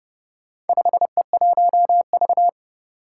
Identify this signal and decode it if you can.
Morse code, heard as a keyed tone, about 30 wpm, 700 Hz. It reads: 5E1V
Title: Four Challenging Letter-Number Combinations 30wpm